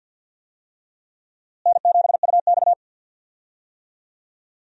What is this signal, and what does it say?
Morse code, heard as a keyed tone, about 53 wpm, 700 Hz. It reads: N7UX
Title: USA Call Signs - 4 Characters - Repeated 2x - 50wpm